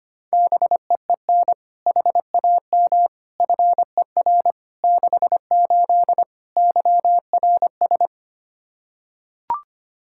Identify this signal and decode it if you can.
Morse code, heard as a keyed tone, about 25 wpm, 700 Hz. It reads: BEEN HAM FER 68 YRS
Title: Practice QSO - Exchange 3 - More Info - 25wpm